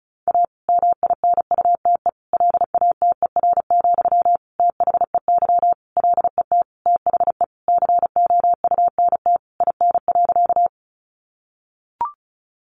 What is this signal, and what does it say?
Morse code, heard as a keyed tone, about 34 wpm, 700 Hz. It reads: A MINUTE LATER, THEY LET THE COUNT IN.
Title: Sentences from Top 1500 Words 35wpm